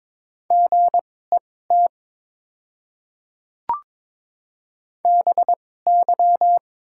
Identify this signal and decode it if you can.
Morse code, heard as a keyed tone, about 22 wpm, spaced as for 16 wpm, 700 Hz. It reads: GET BY
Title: 2x Character Spacing - Top 500 Words - 22wpm